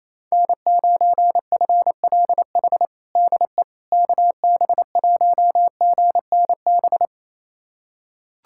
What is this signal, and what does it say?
Morse code, heard as a keyed tone, about 28 wpm, 700 Hz. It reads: N9FLH DE KB1GNB
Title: USA Callsign DE Callsign 28wpm